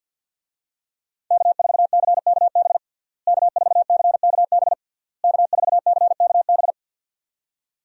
Morse code, heard as a keyed tone, about 47 wpm, 700 Hz. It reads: K4CKB K4CKB K4CKB
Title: USA Call Signs - 5 Characters - Repeated 3x - 50wpm